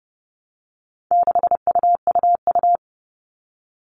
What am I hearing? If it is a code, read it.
Morse code, heard as a keyed tone, about 30 wpm, 700 Hz. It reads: 6UUU
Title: All Cut Numbers - 4-Digit 30wpm